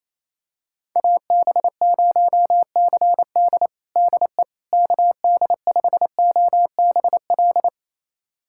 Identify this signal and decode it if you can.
Morse code, heard as a keyed tone, about 27 wpm, 700 Hz. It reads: AB0CD DE KD5OBL
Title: USA Callsign DE Callsign 28wpm